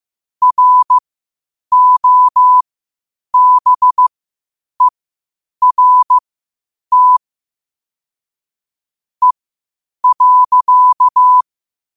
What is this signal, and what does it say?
Morse code, heard as a keyed tone, about 15 wpm, spaced as for 8 wpm, 1000 Hz. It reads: ROBERT E.
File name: qotdfiles / 8wpm